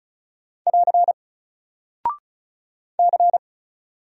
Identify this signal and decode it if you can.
Morse code, heard as a keyed tone, about 36 wpm, 700 Hz. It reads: <AR> C
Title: Single Character (Letter:Number:Punctuation:Prosign) 35wpm